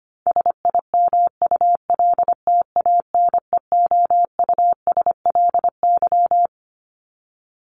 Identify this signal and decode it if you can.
Morse code, heard as a keyed tone, about 25 wpm, 700 Hz. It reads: SIMULTANEOUSLY
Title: LY Suffix - Top 100 Words with Base Words 25wpm